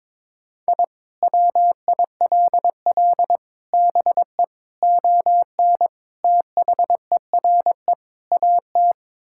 Morse code, heard as a keyed tone, about 22 wpm, 700 Hz. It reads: I WILL BE ON THERE AT
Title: Sentences from Top 1000 Words - Review Part 5 - 22wpm